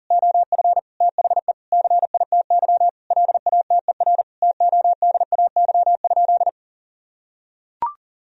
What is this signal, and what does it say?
Morse code, heard as a keyed tone, about 40 wpm, 700 Hz. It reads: OF THE CITY LATER TODAY?
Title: Sentences from Top 1000 Words - Review Part 2 - 40wpm